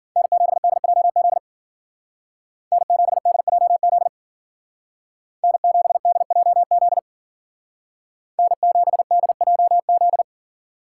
Morse code, heard as a keyed tone, about 52 wpm, 700 Hz. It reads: N7DJZ N7DJZ N7DJZ N7DJZ